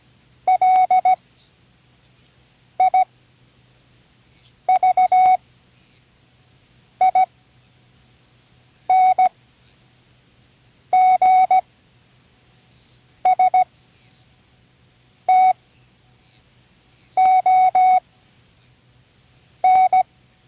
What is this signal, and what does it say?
Morse code, heard as a keyed tone, about 17 wpm, spaced as for 5 wpm, 740 Hz. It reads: LIVINGSTON
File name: CWResources / Learning Morse Code (from Tune in the World) / 29 words using all letters